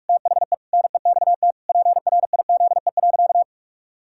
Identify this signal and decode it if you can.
Morse code, heard as a keyed tone, about 46 wpm, 700 Hz. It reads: THE NEXT PRIZE.
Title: Pangram Challenge v2 - 45wpm